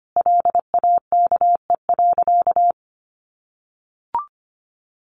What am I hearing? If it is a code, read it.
Morse code, heard as a keyed tone, about 25 wpm, 700 Hz. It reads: LAKE.
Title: Sentences from Top 1100 Words 25wpm